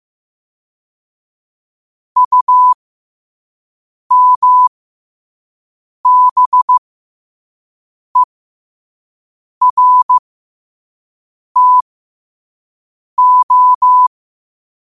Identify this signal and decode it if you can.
Morse code, heard as a keyed tone, about 15 wpm, spaced as for 5 wpm, 1000 Hz. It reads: UMBERTO
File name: qotdfiles / 5wpm